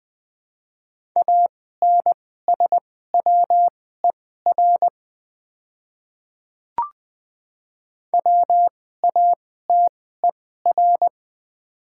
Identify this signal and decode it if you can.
Morse code, heard as a keyed tone, about 20 wpm, spaced as for 14 wpm, 700 Hz. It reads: ANSWER WATER
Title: 2x Character Spacing - Top 300 Words - 20wpm